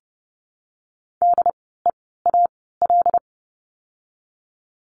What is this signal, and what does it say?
Morse code, heard as a keyed tone, about 30 wpm, spaced as for 17 wpm, 700 Hz. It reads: DEAL
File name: Lesson 16f - Letter c - Words 30wpm